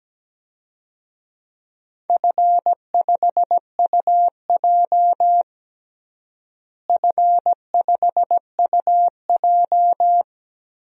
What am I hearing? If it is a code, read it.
Morse code, heard as a keyed tone, about 17 wpm, 700 Hz. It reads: F5UJ F5UJ
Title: DXCC-100 Call Signs - 4 Characters - Repeated 2x - 17wpm